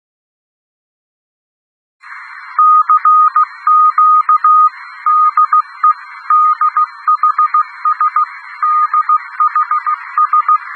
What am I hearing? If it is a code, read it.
Morse code, heard as a keyed tone, about 16 wpm, 1230 Hz. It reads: CQ DE DHS DHS
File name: europe / d-2816-ruegen radio dhs-whe-1992